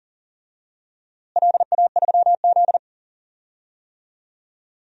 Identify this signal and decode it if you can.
Morse code, heard as a keyed tone, about 40 wpm, 700 Hz. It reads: LA3Z
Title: DXCC-100 Call Signs - 4 Characters - Repeated 3x - 40wpm